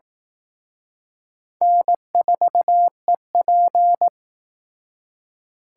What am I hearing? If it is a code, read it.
Morse code, heard as a keyed tone, about 19 wpm, 700 Hz. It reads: N4EP
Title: Speed Racing - USA 1x2 Call Signs 15wpm